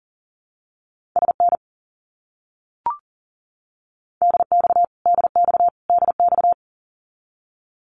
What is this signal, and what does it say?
Morse code, heard as a keyed tone, about 40 wpm, 700 Hz. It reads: SN DX DX DX
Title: Abbreviations Repeated 3x - 40wpm